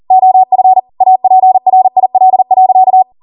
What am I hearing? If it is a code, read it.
Morse code, heard as a keyed tone, about 40 wpm, 750 Hz. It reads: OF APRIL.